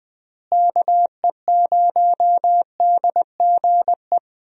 Morse code, heard as a keyed tone, about 20 wpm, 700 Hz. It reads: KE0DGE
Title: USA Call Signs - 6 Characters - Repeated 2x - 20wpm